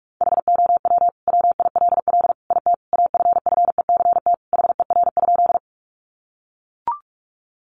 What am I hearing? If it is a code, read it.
Morse code, heard as a keyed tone, about 44 wpm, 700 Hz. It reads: HOW WILL IT AFFECT HER?
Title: Sentences from Top 900 Words 45wpm